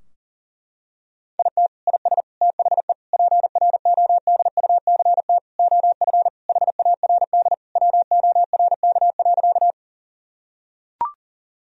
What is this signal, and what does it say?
Morse code, heard as a keyed tone, about 40 wpm, 700 Hz. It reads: IT IS THE PRODUCT OF HARD WORK.
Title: Sentences from Top 1000 Words - Review Part 2 - 40wpm